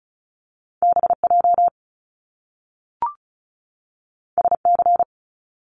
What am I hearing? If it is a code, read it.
Morse code, heard as a keyed tone, about 36 wpm, 700 Hz. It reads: BJ SC
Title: Top Two Letter Combinations in Words 35wpm